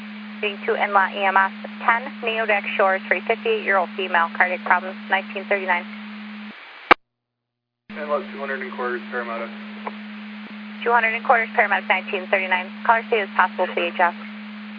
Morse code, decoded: none

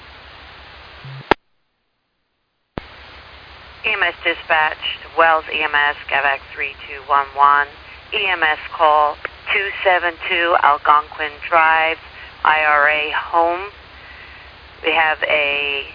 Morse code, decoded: none

